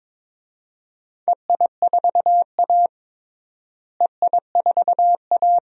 Morse code, heard as a keyed tone, about 21 wpm, 700 Hz. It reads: EI4A EI4A
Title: DXCC-100 Call Signs - 4 Characters - Repeated 2x - 22wpm